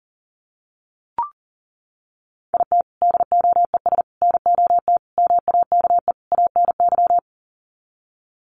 Morse code, heard as a keyed tone, about 40 wpm, 700 Hz. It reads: IT DOES NOT MAKE ANY
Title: Sets of 5 Words Encore2 40wpm